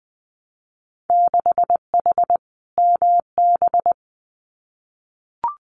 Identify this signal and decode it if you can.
Morse code, heard as a keyed tone, about 20 wpm, 700 Hz. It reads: 6H MB
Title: Winter Field Day Exchanges Encore 20wpm